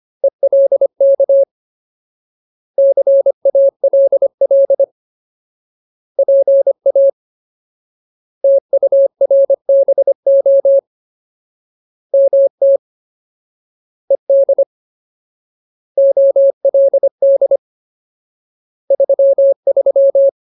Morse code, heard as a keyed tone, about 25 wpm, 550 Hz. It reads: ELK CALL PA TURBO MT ED OLD 33
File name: Ultimate Word List / 25WPM Ultimate